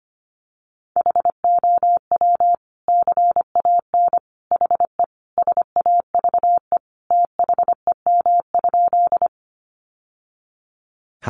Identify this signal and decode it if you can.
Morse code, heard as a keyed tone, about 25 wpm, 700 Hz. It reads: HOW CAN HE SAVE THEM?
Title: Sentences from Top 1100 Words 25wpm